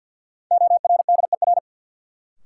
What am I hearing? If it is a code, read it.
Morse code, heard as a keyed tone, about 53 wpm, 700 Hz. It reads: ORDER
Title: Sentences from Top 400 Words - New Word Drill - 50wpm